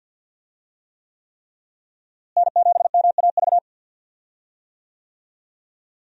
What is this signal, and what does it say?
Morse code, heard as a keyed tone, about 47 wpm, 700 Hz. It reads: N7MAV
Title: USA Call Signs - 5 Characters - Repeated 3x - 50wpm